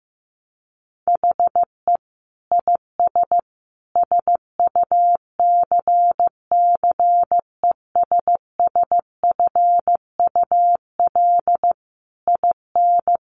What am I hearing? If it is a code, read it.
Morse code, heard as a keyed tone, about 15 wpm, 700 Hz. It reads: HE IS SUCCESSFUL IN
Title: Sentences from Top 900 Words 15wpm